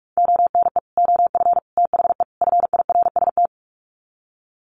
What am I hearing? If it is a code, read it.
Morse code, heard as a keyed tone, about 44 wpm, 700 Hz. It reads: ONE OF THE FIRST
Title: Sets of 4 Words 45wpm